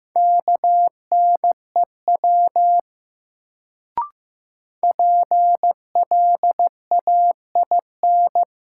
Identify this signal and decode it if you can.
Morse code, heard as a keyed tone, about 15 wpm, 700 Hz. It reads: KNEW PLAIN
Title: Sentences from Top 400 Words - New Word Drill - 15wpm